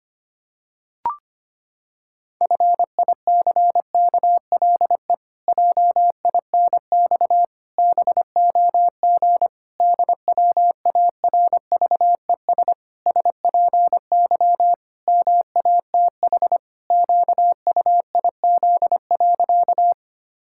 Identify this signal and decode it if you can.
Morse code, heard as a keyed tone, about 25 wpm, 700 Hz. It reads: FICKLE JINX BOG DWARVES SPY MATH QUIZ.
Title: Pangram Challenge v2 - 25wpm